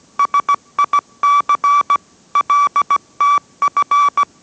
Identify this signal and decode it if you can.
Morse code, heard as a keyed tone, about 17 wpm, 1150 Hz. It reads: SIC LTF